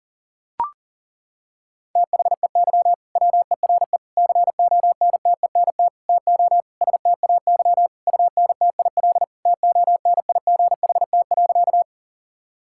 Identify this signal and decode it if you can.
Morse code, heard as a keyed tone, about 40 wpm, 700 Hz. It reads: THEY WERE CONTENT TO STAY UNTIL TONIGHT.
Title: Sentences from Top 1000 Words - Review Part 6 - 40wpm